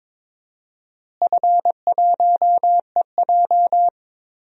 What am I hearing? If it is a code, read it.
Morse code, heard as a keyed tone, about 22 wpm, 700 Hz. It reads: F1EJ
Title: DXCC-100 Call Signs - 4 Characters - Repeated 3x - 22wpm